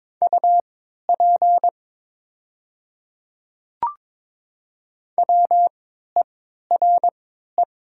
Morse code, heard as a keyed tone, about 22 wpm, spaced as for 13 wpm, 700 Hz. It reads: UP WERE
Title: 3x Character Spacing - Top 100 Words - 22wpm